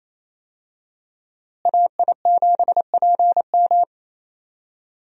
Morse code, heard as a keyed tone, about 28 wpm, 700 Hz. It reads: AI7PM
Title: USA Call Signs - 5 Characters - Repeated 3x - 28wpm